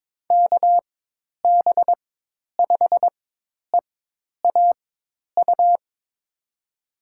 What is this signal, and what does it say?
Morse code, heard as a keyed tone, about 22 wpm, spaced as for 10 wpm, 700 Hz. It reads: KB5EAU
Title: USA Call Signs - 6 Characters - 4x Character Spacing - 22wpm